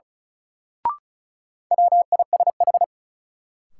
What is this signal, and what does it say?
Morse code, heard as a keyed tone, about 36 wpm, 700 Hz. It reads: WISH
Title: Sentences from Top 1600 Words - New Word Drill Alt - 35wpm